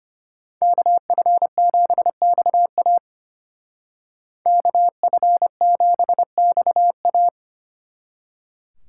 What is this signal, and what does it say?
Morse code, heard as a keyed tone, about 31 wpm, 700 Hz. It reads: KF7XA KF7XA
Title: Speed Racing - USA 2x2 Call Signs 25wpm